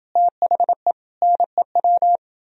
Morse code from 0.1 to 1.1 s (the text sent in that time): THE